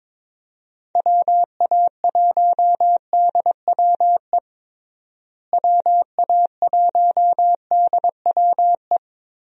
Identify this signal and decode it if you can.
Morse code, heard as a keyed tone, about 22 wpm, 700 Hz. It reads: WA1DWE WA1DWE
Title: USA Call Signs - 6 Characters - Repeated 3x - 22wpm